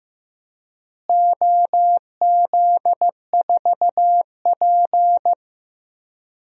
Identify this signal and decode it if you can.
Morse code, heard as a keyed tone, about 15 wpm, 700 Hz. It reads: OZ4P
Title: DXCC-100 Call Signs - 4 Characters - Repeated 2x - 15wpm